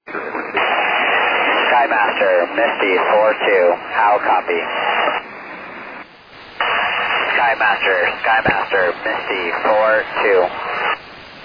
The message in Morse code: none

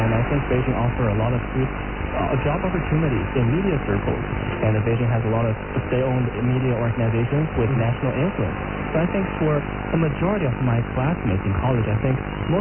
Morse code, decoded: none

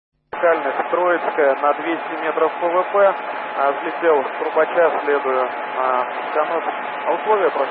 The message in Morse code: none

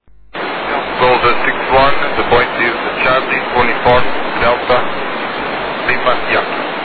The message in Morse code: none

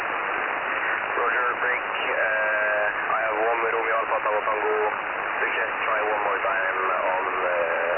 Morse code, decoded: none

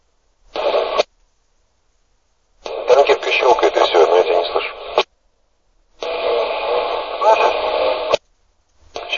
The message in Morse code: none

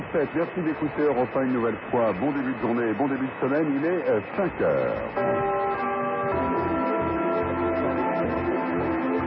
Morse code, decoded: none

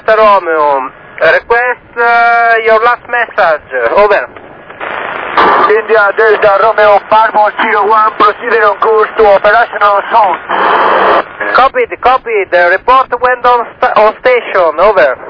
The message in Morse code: none

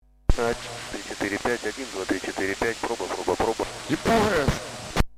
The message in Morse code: none

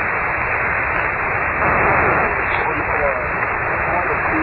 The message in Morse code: none

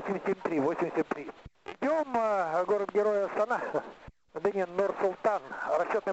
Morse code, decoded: none

